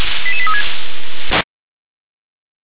Five-tone selective call (ZVEI), heard as quick runs of five tones, tones of about 70 ms. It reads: 88836